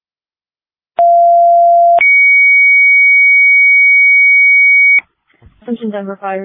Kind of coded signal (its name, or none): two-tone page